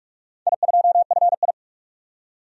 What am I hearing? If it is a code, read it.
Morse code, heard as a keyed tone, about 46 wpm, 700 Hz. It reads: I2FI
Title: DXCC-100 Call Signs - 4 Characters - 45wpm